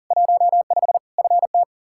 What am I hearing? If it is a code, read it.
Morse code, heard as a keyed tone, about 37 wpm, 700 Hz. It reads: 15 FT
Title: Practice QSO - Exchange 2 - Info - 40wpm